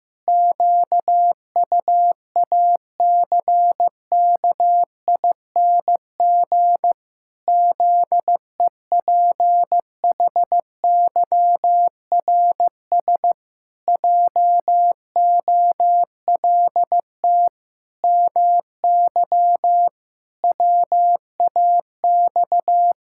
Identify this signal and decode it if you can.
Morse code, heard as a keyed tone, about 15 wpm, 700 Hz. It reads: QUACKING ZEPHYRS JOLT MY WAX
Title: Pangram Challenge v2 - 15wpm